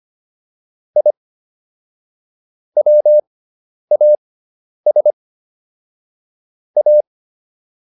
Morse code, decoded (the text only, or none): I WAS A